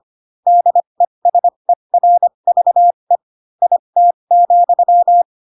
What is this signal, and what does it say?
Morse code, heard as a keyed tone, about 25 wpm, spaced as for 22 wpm, 700 Hz. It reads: DESERVE IT,
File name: Pinocchio / Pinocchio ch3 18 25 ik1hge